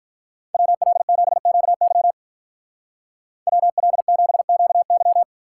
Morse code, heard as a keyed tone, about 44 wpm, 700 Hz. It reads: WL7QY WL7QY